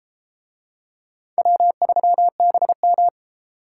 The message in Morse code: W3BM

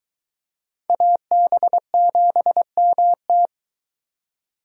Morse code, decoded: AB7MT

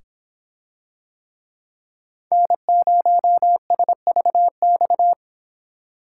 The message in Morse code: N0SVX

